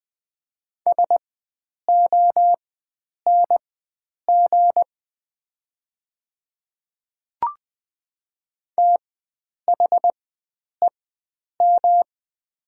Morse code, heard as a keyed tone, about 20 wpm, spaced as for 9 wpm, 700 Hz. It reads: SONG THEM
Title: 4x Character Spacing - Top 500 Words - 20wpm